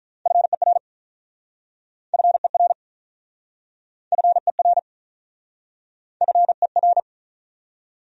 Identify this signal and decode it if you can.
Morse code, heard as a keyed tone, about 47 wpm, 700 Hz. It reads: FER FER FER FER